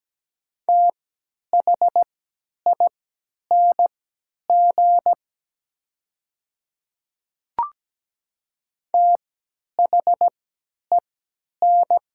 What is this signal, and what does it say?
Morse code, heard as a keyed tone, about 17 wpm, spaced as for 10 wpm, 700 Hz. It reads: THING THEN
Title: 3x Character Spacing - Top 200 Words - 17wpm